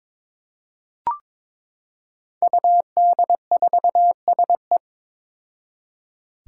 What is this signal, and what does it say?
Morse code, heard as a keyed tone, about 22 wpm, 700 Hz. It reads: UD4SE
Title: Five Challenging Letter-Number Combinations 22wpm